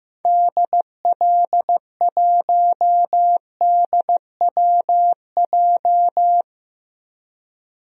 Morse code, heard as a keyed tone, about 16 wpm, 700 Hz. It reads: DL1DWJ
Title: DXCC-100 Call Signs - 6 Characters - 15wpm